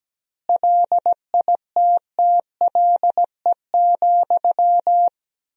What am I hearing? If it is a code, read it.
Morse code, heard as a keyed tone, about 17 wpm, 700 Hz. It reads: LITTLE,
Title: 2x Word-Spacing - Sentences from Top 400 Words - 17wpm